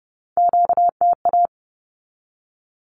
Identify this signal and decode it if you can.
Morse code, heard as a keyed tone, about 30 wpm, 700 Hz. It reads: QTA
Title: Q-Codes 30wpm